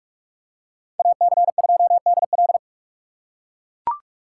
Morse code, heard as a keyed tone, about 44 wpm, 700 Hz. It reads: AC2DL